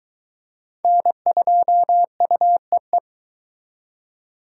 Morse code, heard as a keyed tone, about 23 wpm, 700 Hz. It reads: N2UEE